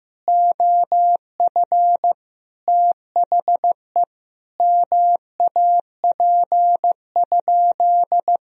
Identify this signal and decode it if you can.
Morse code, heard as a keyed tone, about 15 wpm, 700 Hz. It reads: OF THE MAP?
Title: Sentences from Top 1000 Words - Review Part 2 - 15wpm